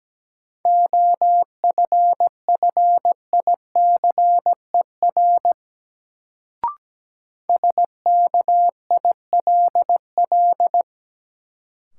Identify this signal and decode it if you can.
Morse code, heard as a keyed tone, about 17 wpm, 700 Hz. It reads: OFFICER SKILL
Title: Sentences from Top 1200 Words - New Word Drill - 17wpm